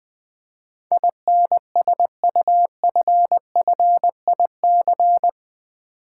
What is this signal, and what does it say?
Morse code, heard as a keyed tone, about 20 wpm, 700 Hz. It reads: INSUFFIC
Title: Build A Word - 4 Letters At A Time - 12-Letter Words - 20wpm